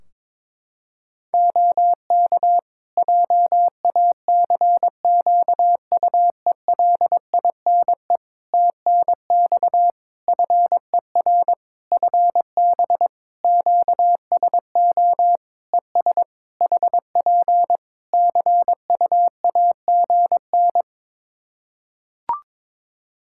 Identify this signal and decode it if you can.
Morse code, heard as a keyed tone, about 22 wpm, 700 Hz. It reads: OK JACQUELINE TNX FER FB QSO ES HP CUAGN
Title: K1USN SST QSO Practice 22wpm